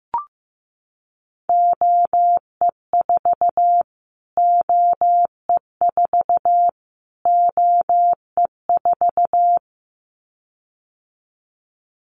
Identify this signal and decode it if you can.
Morse code, heard as a keyed tone, about 15 wpm, 700 Hz. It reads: OE4 OE4 OE4
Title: Callsign Prefixes for Top 25 DXCC - Repeated 3x - 15wpm